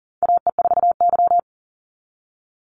Morse code, heard as a keyed tone, about 38 wpm, 700 Hz. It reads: AE4Y
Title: NA SOTA Chaser Callsigns 40wpm